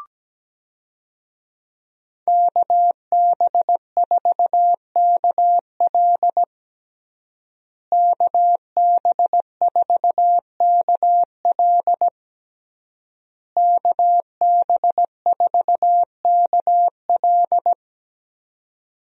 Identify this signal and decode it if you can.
Morse code, heard as a keyed tone, about 17 wpm, 700 Hz. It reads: KB4KL KB4KL KB4KL
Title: USA Call Signs - 5 Characters - Repeated 3x - 17wpm